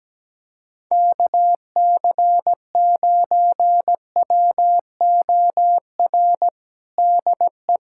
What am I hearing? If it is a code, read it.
Morse code, heard as a keyed tone, about 17 wpm, 700 Hz. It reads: KC9WOR DE
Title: USA Callsign DE Callsign 17wpm